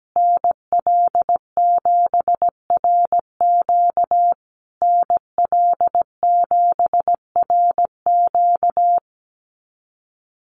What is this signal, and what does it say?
Morse code, heard as a keyed tone, about 17 wpm, 700 Hz. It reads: NL7RQ NL7RQ
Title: USA Call Signs Repeated 3x 17wpm